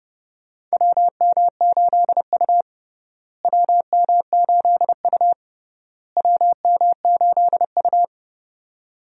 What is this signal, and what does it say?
Morse code, heard as a keyed tone, about 30 wpm, 700 Hz. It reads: WM8U WM8U WM8U
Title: USA Call Signs - 4 Characters - Repeated 3x - 30wpm